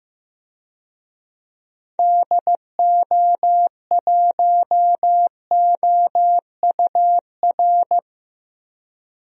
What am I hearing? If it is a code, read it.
Morse code, heard as a keyed tone, about 15 wpm, 700 Hz. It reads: DO1OUR
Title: DXCC-100 Call Signs - 6 Characters - Repeated 2x - 15wpm